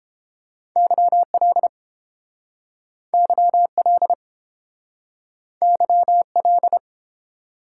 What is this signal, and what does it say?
Morse code, heard as a keyed tone, about 32 wpm, 700 Hz. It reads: YL YL YL